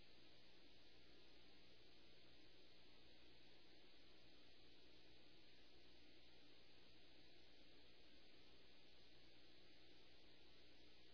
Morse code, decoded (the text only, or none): none